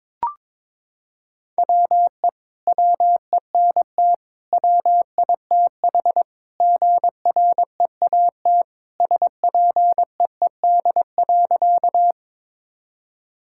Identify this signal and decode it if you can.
Morse code, heard as a keyed tone, about 22 wpm, 700 Hz. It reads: WE WENT WITH GREAT SPEED.
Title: Sentences from Top 1000 Words - Review Part 5 - 22wpm